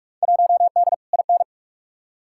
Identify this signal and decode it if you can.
Morse code, heard as a keyed tone, about 43 wpm, 700 Hz. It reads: 1D IN
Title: Field Day Exchanges - 45wpm